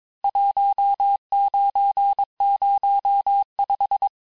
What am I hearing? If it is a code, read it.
Morse code, heard as a keyed tone, about 22 wpm, 800 Hz. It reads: 1905
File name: cours-du-lundi files / cours-1587-texte-clair-vitesse-16mpm-puis-21mpm